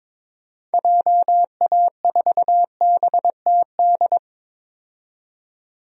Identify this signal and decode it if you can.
Morse code, heard as a keyed tone, about 22 wpm, 700 Hz. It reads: JA4BTD
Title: DXCC-100 Call Signs - 6 Characters - 22wpm